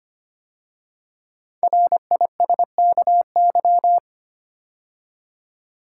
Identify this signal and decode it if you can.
Morse code, heard as a keyed tone, about 25 wpm, 700 Hz. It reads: RISKY
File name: 5 Letter Words - First 3 Letters Given - 25wpm